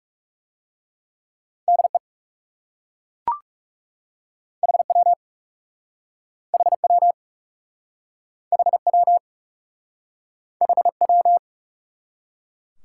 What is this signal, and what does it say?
Morse code, heard as a keyed tone, about 37 wpm, 700 Hz. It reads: DE HW HW HW HW
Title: Speed-Racing Top 90 QSO Elements 30wpm